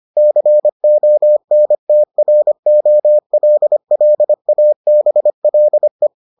Morse code, heard as a keyed tone, about 25 wpm, 600 Hz. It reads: CONTROLLABLE